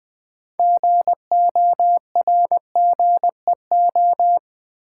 Morse code, heard as a keyed tone, about 20 wpm, 700 Hz. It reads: GORGEO
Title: Build A Word - 2 Letters At A Time - 8-Letter Words - 20wpm